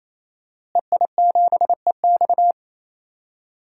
Morse code, heard as a keyed tone, about 28 wpm, 700 Hz. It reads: EI7EX
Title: DXCC-100 Call Signs - 5 Characters - Repeated 3x - 28wpm